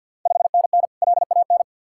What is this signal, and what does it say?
Morse code, heard as a keyed tone, about 53 wpm, 700 Hz. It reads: 5NN LAN
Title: Ontario QSO Party Exchange 50wpm